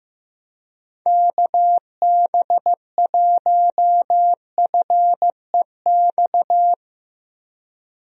Morse code, heard as a keyed tone, about 15 wpm, 700 Hz. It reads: KB1FEX
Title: USA Call Signs - 6 Characters - Repeated 2x - 15wpm